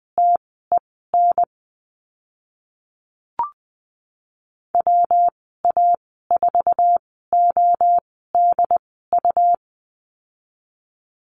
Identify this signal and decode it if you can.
Morse code, heard as a keyed tone, about 20 wpm, spaced as for 15 wpm, 700 Hz. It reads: TEN WA4ODU
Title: 140 Lesson 23g - Letter f - Call Signs 20wpm